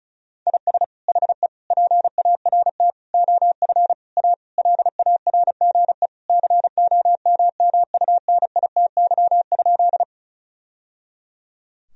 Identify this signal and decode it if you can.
Morse code, heard as a keyed tone, about 36 wpm, 700 Hz. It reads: IS HE PART OF A LARGE COMMUNITY?
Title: Sentences from Top 1000 Words - Review Part 4 - 35wpm